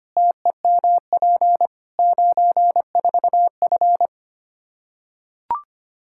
Morse code, heard as a keyed tone, about 25 wpm, 700 Hz. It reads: TEMP 94F
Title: Practice QSO - Exchange 2 - Info - 25wpm